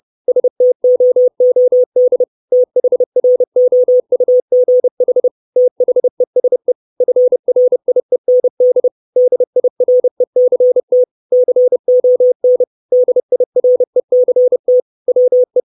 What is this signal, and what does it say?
Morse code, heard as a keyed tone, about 30 wpm, 500 Hz. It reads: STOOD THROUGH THESE FRIEND DIRECT CON DIRECT WE
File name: 632 Words 30WPM